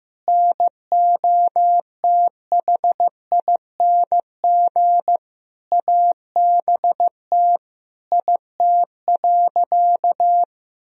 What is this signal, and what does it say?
Morse code, heard as a keyed tone, about 15 wpm, 700 Hz. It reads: NOTHING ABT IT.
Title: Sentences from Top 500 Words and Abbreviations 15wpm